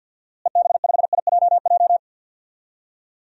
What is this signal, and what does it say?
Morse code, heard as a keyed tone, about 53 wpm, 700 Hz. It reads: EB5IJJ
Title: DXCC-100 Call Signs - 6 Characters - Repeated 2x - 50wpm